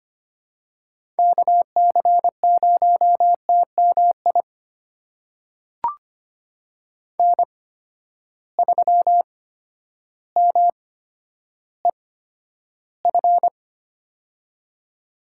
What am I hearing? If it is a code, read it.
Morse code, heard as a keyed tone, about 25 wpm, spaced as for 11 wpm, 700 Hz. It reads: KC0TMI N3MEF
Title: USA Call Signs - 8x Character Spacing - 25wpm